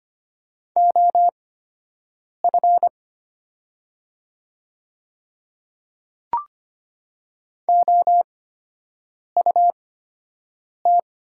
Morse code, heard as a keyed tone, about 25 wpm, spaced as for 7 wpm, 700 Hz. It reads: OF OUT
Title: 8x Character Spacing - Top 100 Words - 25wpm